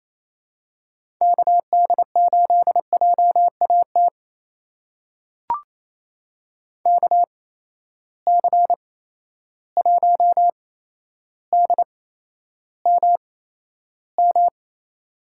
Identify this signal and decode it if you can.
Morse code, heard as a keyed tone, about 28 wpm, spaced as for 12 wpm, 700 Hz. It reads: KD8JAT KC1DMM
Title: USA Call Signs - 6 Characters - 8x Character Spacing - 28wpm